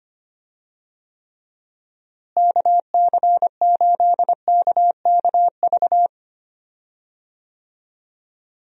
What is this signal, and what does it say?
Morse code, heard as a keyed tone, about 25 wpm, 700 Hz. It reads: KC8KKV
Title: USA Call Signs - 6 Characters - Repeated 2x - 25wpm